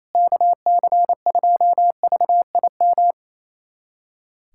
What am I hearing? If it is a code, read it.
Morse code, heard as a keyed tone, about 28 wpm, 700 Hz. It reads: KC2VIM
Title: USA Callsign DE Callsign - 6 Characters - 28wpm